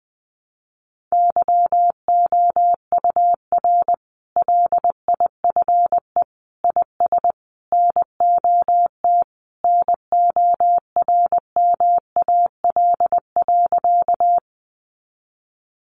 Morse code, heard as a keyed tone, about 20 wpm, 700 Hz. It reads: YOUR LIFE IS NOT NORMAL.
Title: Sentences from Top 1200 Words 20wpm